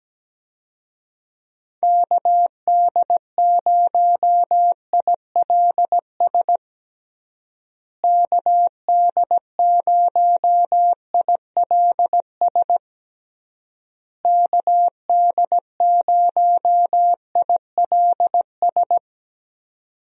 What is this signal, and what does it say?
Morse code, heard as a keyed tone, about 17 wpm, 700 Hz. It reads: KD0ILS KD0ILS KD0ILS
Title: USA Call Signs - 6 Characters - Repeated 3x - 17wpm